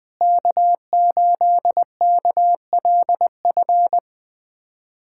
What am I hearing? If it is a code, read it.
Morse code, heard as a keyed tone, about 20 wpm, 700 Hz. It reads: K8KLF
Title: USA Callsign DE Callsign - 5 Characters - 20wpm